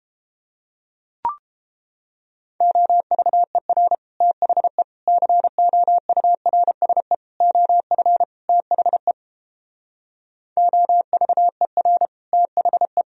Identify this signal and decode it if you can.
Morse code, heard as a keyed tone, about 29 wpm, 700 Hz. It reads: OVER THE COURSE OF THE OVER THE